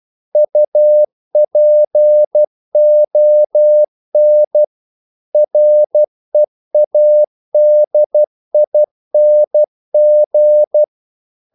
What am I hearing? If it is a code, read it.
Morse code, heard as a keyed tone, about 12 wpm, 600 Hz. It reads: UPON READING
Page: English quotations in CW (telegraphy - morse)